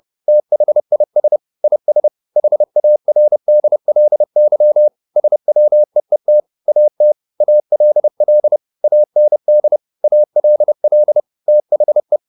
Morse code, heard as a keyed tone, about 30 wpm, 600 Hz. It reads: THIS IS HARDLY SWEET AT ALL AND ALL THE
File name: the old man and the sea / the old man and the sea 1 30wpm